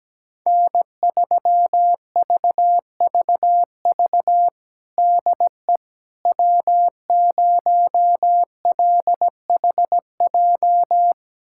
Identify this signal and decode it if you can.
Morse code, heard as a keyed tone, about 17 wpm, 700 Hz. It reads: N3VVV DE W0LHJ
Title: USA Callsign DE Callsign - 5 Characters - 17wpm